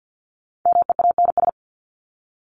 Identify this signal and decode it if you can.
Morse code, heard as a keyed tone, about 50 wpm, 700 Hz. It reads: MEANS
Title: ES:S Suffix - Top 100 Words with Base Words 50wpm